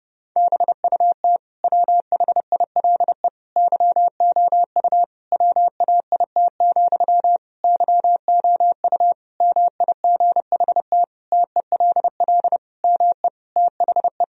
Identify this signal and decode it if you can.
Morse code, heard as a keyed tone, about 30 wpm, 700 Hz. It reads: BUT WHILE YOU WAIT, YOU MIGHT TELL ME THE
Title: Sentences from Top 1000 Words - Review Part 1 - 30wpm